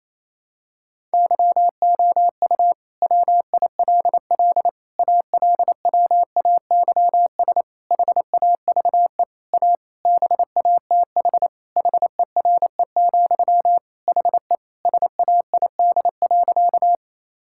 Morse code, heard as a keyed tone, about 28 wpm, 700 Hz. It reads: YOU WILL ALWAYS HAVE A BATH HERE, HE SAID.
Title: Sentences from Top 1600 Words 28wpm